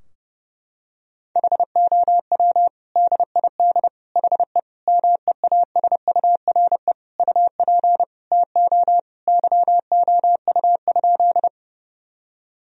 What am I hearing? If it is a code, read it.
Morse code, heard as a keyed tone, about 30 wpm, 700 Hz. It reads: HOW DID HE MEASURE UP TO YOU?
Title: Sentences from Top 1000 Words - Review Part 2 - 30wpm